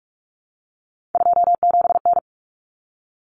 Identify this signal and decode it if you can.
Morse code, heard as a keyed tone, about 44 wpm, 700 Hz. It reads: 27N
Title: Common Cut Numbers - 3-Digit 45wpm